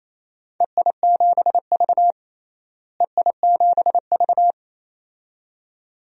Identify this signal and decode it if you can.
Morse code, heard as a keyed tone, about 28 wpm, 700 Hz. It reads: EI7V EI7V
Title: DXCC-100 Call Signs - 4 Characters - Repeated 2x - 28wpm